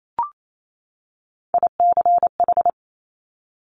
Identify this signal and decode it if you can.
Morse code, heard as a keyed tone, about 28 wpm, 700 Hz. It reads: ICH
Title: Top Three Letter Combinations in Words 28wpm